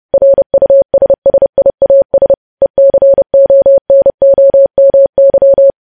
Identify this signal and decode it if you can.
Morse code, heard as a keyed tone, about 30 wpm, 550 Hz. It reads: RUSSIAS ECONOMY
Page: news headlines in morse code at 30 wpm